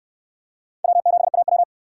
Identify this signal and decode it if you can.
Morse code, heard as a keyed tone, about 68 wpm, 700 Hz. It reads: W7AW